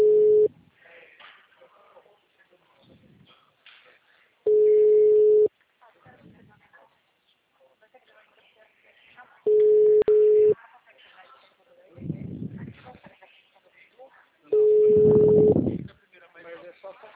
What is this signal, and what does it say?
Ringback tone (Europe, 425 Hz), with speech.